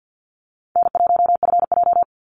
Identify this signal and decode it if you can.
Morse code, heard as a keyed tone, about 53 wpm, 700 Hz. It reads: N1FJ
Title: NA SOTA Activator Callsigns 50wpm